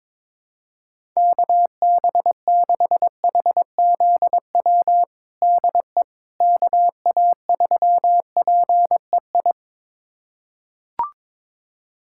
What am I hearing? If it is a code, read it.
Morse code, heard as a keyed tone, about 22 wpm, 700 Hz. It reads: KB6HZW DE KA3PEI
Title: USA Callsign DE Callsign - 6 Characters - 22wpm